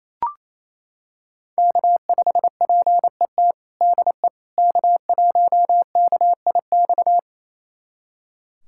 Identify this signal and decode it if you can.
Morse code, heard as a keyed tone, about 28 wpm, 700 Hz. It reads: K5PET DE K1KIX
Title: USA Callsign DE Callsign - 5 Characters - 28wpm